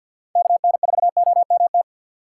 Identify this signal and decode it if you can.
Morse code, heard as a keyed tone, about 48 wpm, 700 Hz. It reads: KN4OMT
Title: USA Call Signs - 6 Characters - 8x Character Spacing - 50wpm